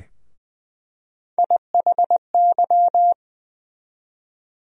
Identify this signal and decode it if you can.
Morse code, heard as a keyed tone, about 20 wpm, 700 Hz. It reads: IHY
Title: USA Call Sign Postfixes - 3 Letters 20wpm